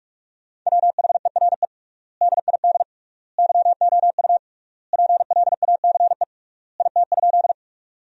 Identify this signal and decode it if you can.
Morse code, heard as a keyed tone, about 41 wpm, 700 Hz. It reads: WHERE DID YOU PLACE IT?
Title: 3x Word-Spacing - Sentences from Top 200 Words - 45wpm